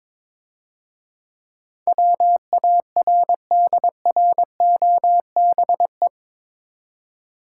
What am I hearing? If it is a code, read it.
Morse code, heard as a keyed tone, about 23 wpm, 700 Hz. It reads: WARDROBE
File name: Easy Words - Speed-Racing with Hints - 22wpm